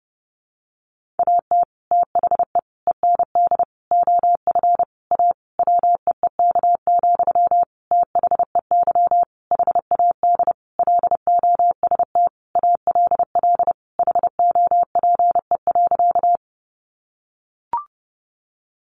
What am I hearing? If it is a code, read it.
Morse code, heard as a keyed tone, about 30 wpm, 700 Hz. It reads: AT THE END OF A WEEK, THEY HAD LOST ALL HOPE.
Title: Sentences from Top 1400 Words 30wpm